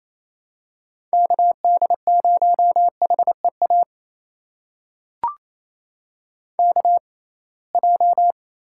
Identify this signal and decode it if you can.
Morse code, heard as a keyed tone, about 27 wpm, spaced as for 21 wpm, 700 Hz. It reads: KD0HEA KJ
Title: USA Call Signs - 6 Characters - 6x Character Spacing - 28wpm